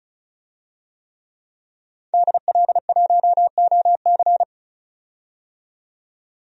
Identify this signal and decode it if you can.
Morse code, heard as a keyed tone, about 34 wpm, 700 Hz. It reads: DL1OC
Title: DXCC-100 Call Signs - 5 Characters - 35wpm